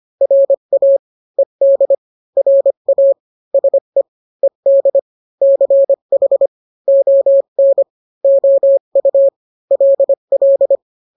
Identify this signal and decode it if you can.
Morse code, heard as a keyed tone, about 25 wpm, 550 Hz. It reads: RA ED RA SE ED CH ON OU LL